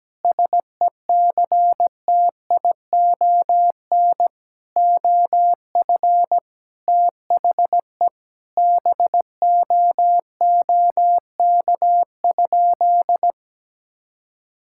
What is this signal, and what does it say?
Morse code, heard as a keyed tone, about 17 wpm, 700 Hz. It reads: SECTION OF THE BOOK?
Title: Sentences from Top 1000 Words - Review Part 5 - 17wpm